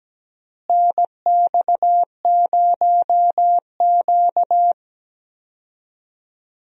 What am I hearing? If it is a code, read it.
Morse code, heard as a keyed tone, about 17 wpm, 700 Hz. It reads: NX0Q
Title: USA Call Signs - 4 Characters - Repeated 3x - 17wpm